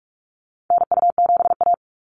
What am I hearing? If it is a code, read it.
Morse code, heard as a keyed tone, about 44 wpm, 700 Hz. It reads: NU7A
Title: NA SOTA Activator Callsigns 45wpm